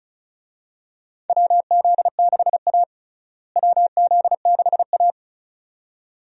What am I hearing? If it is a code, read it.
Morse code, heard as a keyed tone, about 36 wpm, 700 Hz. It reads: WZ6A WZ6A
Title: USA Call Signs - 4 Characters - Repeated 2x - 35wpm